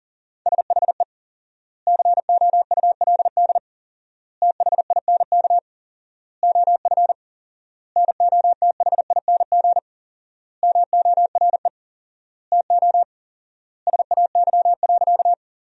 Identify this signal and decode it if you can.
Morse code, heard as a keyed tone, about 37 wpm, 700 Hz. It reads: SHE COULD THINK OF NOTHING MORE TO SAY.
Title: 4x Word-Spacing - Sentences from Top 500 Words - 40wpm